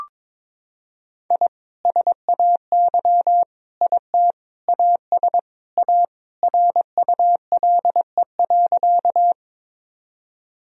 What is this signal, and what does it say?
Morse code, heard as a keyed tone, about 22 wpm, 700 Hz. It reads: I SAY IT AS A RULE.
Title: Sentences from Top 1000 Words - Review Part 2 - 22wpm